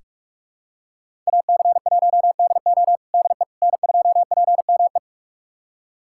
Morse code, heard as a keyed tone, about 45 wpm, 700 Hz. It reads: AC1DO DE N2PME
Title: USA Callsign DE Callsign - 5 Characters - 45wpm